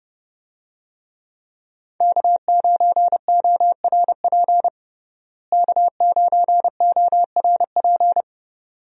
Morse code, heard as a keyed tone, about 30 wpm, 700 Hz. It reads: K9ORP K9ORP
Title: USA Call Signs - 5 Characters - Repeated 2x - 30wpm